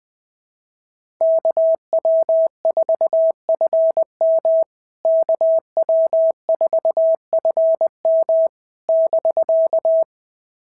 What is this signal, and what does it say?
Morse code, heard as a keyed tone, about 21 wpm, 650 Hz. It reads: KW4FM KW4FM <BK>
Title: POTA QSO Practice - 20wpm